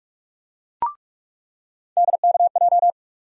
Morse code, heard as a keyed tone, about 45 wpm, 700 Hz. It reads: DKJ
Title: USA Call Sign Postfixes - 3 Letters 45wpm